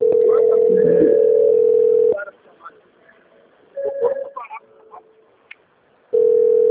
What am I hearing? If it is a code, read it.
Ringback tone (North America), 440 + 480 Hz, with speech.
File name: out- - - - - .1573